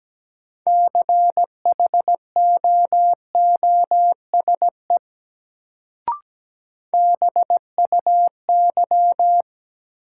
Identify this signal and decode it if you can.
Morse code, heard as a keyed tone, about 17 wpm, 700 Hz. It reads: CHOOSE BUY
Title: Sentences from Top 700 Words - New Word Drill - 17wpm